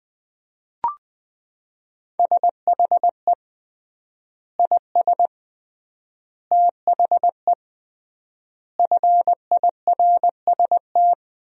Morse code, heard as a keyed tone, about 21 wpm, 700 Hz. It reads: SHE IS THE FIRST
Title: 3x Word-Spacing - Sentences from Top 400 Words - 20wpm